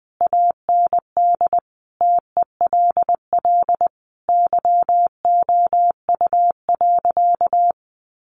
Morse code, heard as a keyed tone, about 20 wpm, 700 Hz. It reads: AND TELL YOU.
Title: Sentences from Top 1100 Words 20wpm